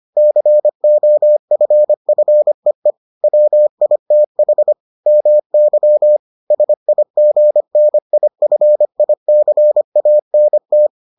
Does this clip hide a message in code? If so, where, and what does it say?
Yes: Morse code from 0.2 to 10.9 s, heard as a keyed tone, about 25 wpm, 600 Hz. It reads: COFFEE WITH MY SIGNIFICANT